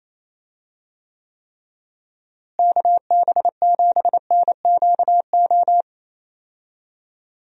Morse code, heard as a keyed tone, about 28 wpm, 700 Hz. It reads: KB7NQO
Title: USA Call Signs - 6 Characters - Repeated 3x - 28wpm